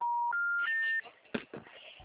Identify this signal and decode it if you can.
Special information tone.
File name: out- -2013- - - .5116